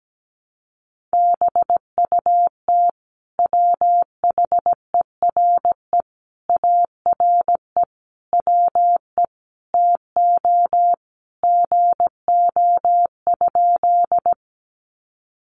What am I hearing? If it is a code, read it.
Morse code, heard as a keyed tone, about 17 wpm, 700 Hz. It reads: BUT WHERE ARE WE TO GO?
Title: Sentences from Top 200 Words 17wpm